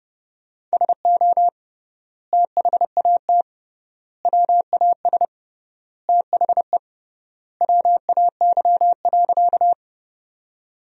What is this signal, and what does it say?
Morse code, heard as a keyed tone, about 30 wpm, 700 Hz. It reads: SO THAT WAS THE WAY.
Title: 3x Word-Spacing - Sentences from Top 100 Words - 30wpm